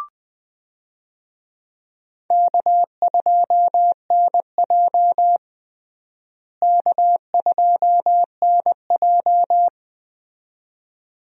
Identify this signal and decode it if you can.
Morse code, heard as a keyed tone, about 21 wpm, 700 Hz. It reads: K2NJ K2NJ
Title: USA Call Signs - 4 Characters - Repeated 2x - 20wpm